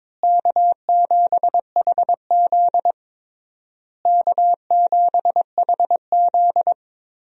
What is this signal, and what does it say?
Morse code, heard as a keyed tone, about 22 wpm, 700 Hz. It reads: K7HZ K7HZ
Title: USA Call Signs - 4 Characters - Repeated 3x - 22wpm